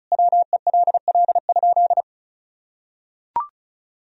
Morse code, heard as a keyed tone, about 36 wpm, 700 Hz. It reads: WELL?
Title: Sentences from Top 1000 Words - Review Part 2 - 35wpm